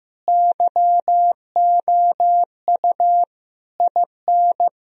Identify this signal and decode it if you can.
Morse code, heard as a keyed tone, about 16 wpm, 700 Hz. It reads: YOU IN
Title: Sentences from Top 1000 Words - Review Part 2 - 15wpm